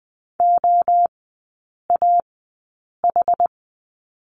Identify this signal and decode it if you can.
Morse code, heard as a keyed tone, about 20 wpm, spaced as for 8 wpm, 700 Hz. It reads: OAH
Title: 100 Lesson 17e - Letter u - Three Characters 20wpm